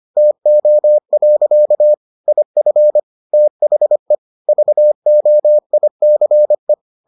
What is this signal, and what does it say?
Morse code, heard as a keyed tone, about 25 wpm, 600 Hz. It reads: TO. IF THE VOICE